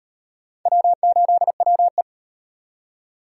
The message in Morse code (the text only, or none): W8WE